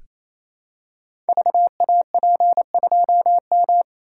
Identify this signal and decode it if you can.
Morse code, heard as a keyed tone, about 28 wpm, 700 Hz. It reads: VAP2M
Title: Code Groups - 5 Letter:Numbers - 28wpm